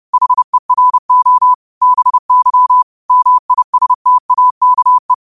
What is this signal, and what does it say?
Morse code, heard as a keyed tone, about 30 wpm, 1000 Hz. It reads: HERO BY MISTAKE
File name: qotdfiles / 30wpm